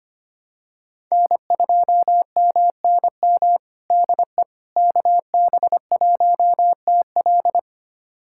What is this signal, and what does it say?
Morse code, heard as a keyed tone, about 25 wpm, 700 Hz. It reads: N2MNM DE KB1TL
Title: USA Callsign DE Callsign - 5 Characters - 25wpm